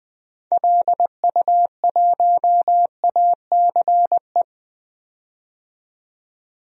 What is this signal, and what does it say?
Morse code, heard as a keyed tone, about 20 wpm, 700 Hz. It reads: LU1ACE
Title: DXCC-100 Call Signs - 6 Characters - 20wpm